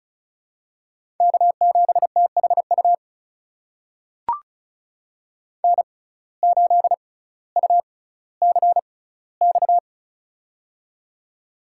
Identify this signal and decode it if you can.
Morse code, heard as a keyed tone, about 34 wpm, spaced as for 18 wpm, 700 Hz. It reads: K7THU N8UCX
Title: USA Call Signs - 5 Characters - 6x Character Spacing - 35wpm